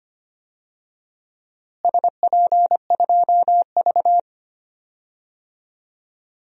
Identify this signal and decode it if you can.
Morse code, heard as a keyed tone, about 25 wpm, 700 Hz. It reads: SP2V
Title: DXCC-100 Call Signs - 4 Characters - Repeated 2x - 25wpm